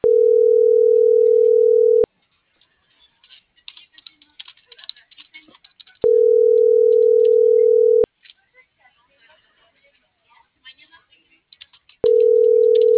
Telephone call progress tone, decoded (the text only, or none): ringback tone